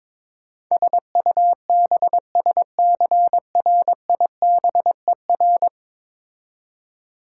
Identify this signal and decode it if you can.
Morse code, heard as a keyed tone, about 22 wpm, 700 Hz. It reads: SUBSCRIBER
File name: practice / 22wpm / 10 Letter Words - First 3 Letters Given - 22wpm